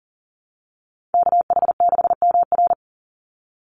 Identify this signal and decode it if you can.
Morse code, heard as a keyed tone, about 40 wpm, 700 Hz. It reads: KH6MR
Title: USA Call Signs 40wpm